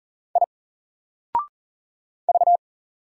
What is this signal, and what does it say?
Morse code, heard as a keyed tone, about 40 wpm, 700 Hz. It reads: I V
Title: Single Letter-Number 40wpm